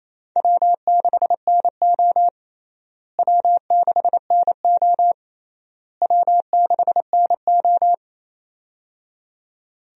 Morse code, heard as a keyed tone, about 28 wpm, 700 Hz. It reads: W6NO W6NO W6NO
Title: USA Call Signs - 4 Characters - Repeated 3x - 28wpm